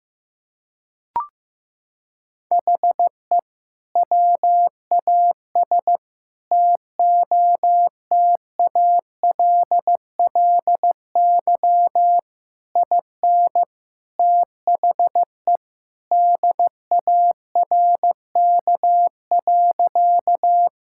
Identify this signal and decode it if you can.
Morse code, heard as a keyed tone, about 15 wpm, 700 Hz. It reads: HE WAS TOTALLY IN THE DARK.
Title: Sentences from Top 1600 Words 15wpm